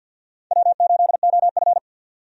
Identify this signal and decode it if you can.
Morse code, heard as a keyed tone, about 52 wpm, 700 Hz. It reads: W8OF
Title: USA Call Signs - 4 Characters - 50wpm